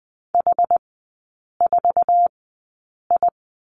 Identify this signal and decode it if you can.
Morse code, heard as a keyed tone, about 20 wpm, spaced as for 8 wpm, 700 Hz. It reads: H4I
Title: 060 Lesson 11e - Letter h - Three Characters 20wpm